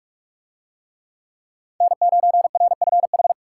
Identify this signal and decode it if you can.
Morse code, heard as a keyed tone, about 42 wpm, 700 Hz. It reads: N9RFH